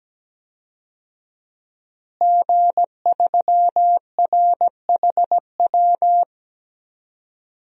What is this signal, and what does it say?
Morse code, heard as a keyed tone, about 17 wpm, 700 Hz. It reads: G3RHW
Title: DXCC-100 Call Signs - 5 Characters - Repeated 3x - 17wpm